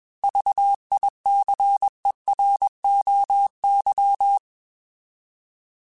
Morse code, heard as a keyed tone, about 21 wpm, 800 Hz. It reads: VICEROY
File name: Alphabetical / Alphabetical mp3s / Vs @20